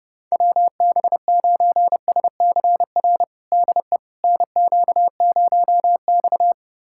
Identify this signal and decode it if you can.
Morse code, heard as a keyed tone, about 32 wpm, 700 Hz. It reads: WB9SCR DE NQ0X
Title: USA Callsign DE Callsign 30wpm